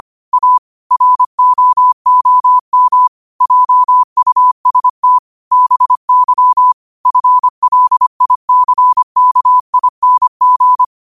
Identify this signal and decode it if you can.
Morse code, heard as a keyed tone, about 25 wpm, 1000 Hz. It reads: A ROOM JUST BY FLICKING